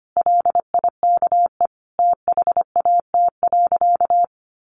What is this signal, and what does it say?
Morse code, heard as a keyed tone, about 25 wpm, 700 Hz. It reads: LIKE THAT.
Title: A Sound Of Thunder 25wpm